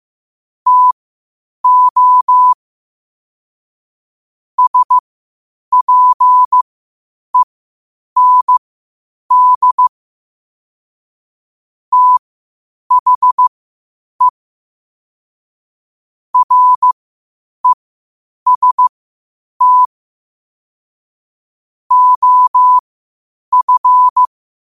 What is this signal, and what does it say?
Morse code, heard as a keyed tone, about 15 wpm, spaced as for 8 wpm, 1000 Hz. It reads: TO SPEND THE REST OF
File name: qotdfiles / 8wpm